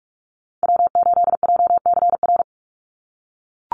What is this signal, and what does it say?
Morse code, heard as a keyed tone, about 44 wpm, 700 Hz. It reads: W8JCR
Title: USA Call Signs 45wpm